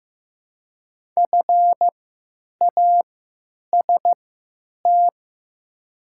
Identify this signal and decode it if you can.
Morse code, heard as a keyed tone, about 15 wpm, spaced as for 9 wpm, 700 Hz. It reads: FAST
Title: 3x Character Spacing - Top 500 Words - 15wpm